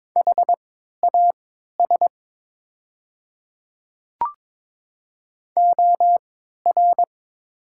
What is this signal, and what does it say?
Morse code, heard as a keyed tone, about 22 wpm, spaced as for 12 wpm, 700 Hz. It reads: HAS OR
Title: 3x Character Spacing - Top 300 Words - 22wpm